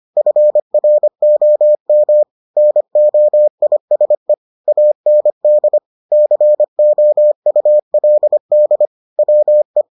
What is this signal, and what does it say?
Morse code, heard as a keyed tone, about 25 wpm, 600 Hz. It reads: FROM NOISE AND COULD WE